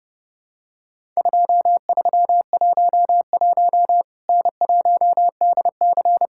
Morse code, heard as a keyed tone, about 30 wpm, 700 Hz. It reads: 2311 N1DC
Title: Worked All Europe DX Contest - QTC Exchange - 30wpm